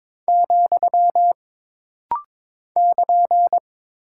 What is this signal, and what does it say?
Morse code, heard as a keyed tone, about 22 wpm, 700 Hz. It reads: , <KN>
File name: practice / 22wpm / Single Punctuation-Prosign - Rapid-Fire - 22wpm